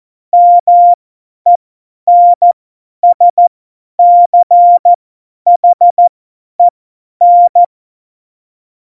Morse code, heard as a keyed tone, about 14 wpm, spaced as for 10 wpm, 700 Hz. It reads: MENSCHEN
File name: Morsekurs / Witze / Witz 15. Dezember 2020, 50 F70